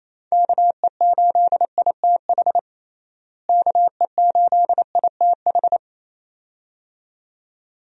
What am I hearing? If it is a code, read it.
Morse code, heard as a keyed tone, about 28 wpm, 700 Hz. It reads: KE8ITH KE8ITH
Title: USA Call Signs - 6 Characters - Repeated 2x - 28wpm